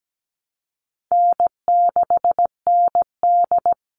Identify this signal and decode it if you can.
Morse code, heard as a keyed tone, about 17 wpm, 700 Hz. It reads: N6ND
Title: All Cut Numbers - 4-Digit 17wpm